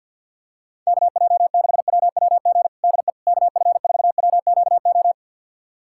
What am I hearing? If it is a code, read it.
Morse code, heard as a keyed tone, about 49 wpm, 700 Hz. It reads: KJ6WWG DE KF4WQO
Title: USA Callsign DE Callsign - 6 Characters - 50wpm